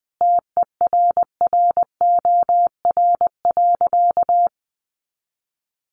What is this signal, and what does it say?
Morse code, heard as a keyed tone, about 20 wpm, 700 Hz. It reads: TERROR.
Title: A Sound Of Thunder 20wpm